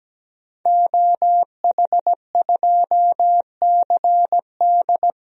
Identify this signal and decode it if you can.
Morse code, heard as a keyed tone, about 17 wpm, 700 Hz. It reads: OH2CD
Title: DXCC-100 Call Signs - 5 Characters - Repeated 2x - 17wpm